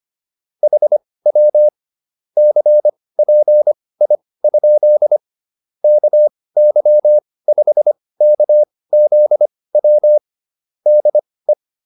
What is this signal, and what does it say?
Morse code, heard as a keyed tone, about 25 wpm, spaced as for 18 wpm, 600 Hz. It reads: HW CPI? KY5KZW DE
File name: QSO304 18